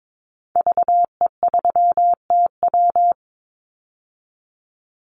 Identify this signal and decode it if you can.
Morse code, heard as a keyed tone, about 22 wpm, 700 Hz. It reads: VE3TW
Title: Call Signs - 22wpm